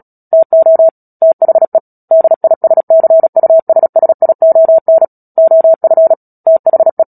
Morse code, heard as a keyed tone, about 37 wpm, 650 Hz. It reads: TO THE DISCUSSION OF THE
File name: Book Two / 10-Chapter 10 - Epilogue - 36 WPM